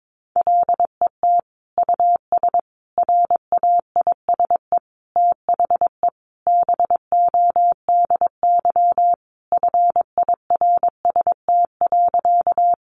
Morse code, heard as a keyed tone, about 22 wpm, 700 Hz. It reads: LET US RAISE THE BODY FIRST.
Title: Sentences from Top 1400 Words 22wpm